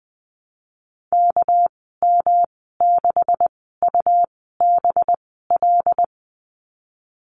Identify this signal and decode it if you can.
Morse code, heard as a keyed tone, about 19 wpm, spaced as for 14 wpm, 700 Hz. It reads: KM6UBL
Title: 207 Lesson 33g - Letter j - Call Signs 20wpm